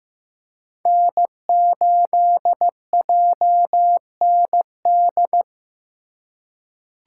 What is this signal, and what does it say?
Morse code, heard as a keyed tone, about 15 wpm, 700 Hz. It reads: N8JND
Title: USA Call Signs - 5 Characters - 3x Character Spacing - 15wpm